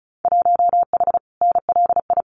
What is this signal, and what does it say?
Morse code, heard as a keyed tone, about 34 wpm, 700 Hz. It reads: 1H NLI
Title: Winter Field Day Exchanges 35wpm